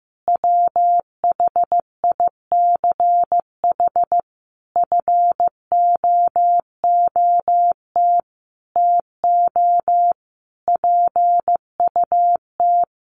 Morse code, heard as a keyed tone, about 15 wpm, 700 Hz. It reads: WHICH FOOT TO PUT
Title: Sentences from Top 1300 Words 15wpm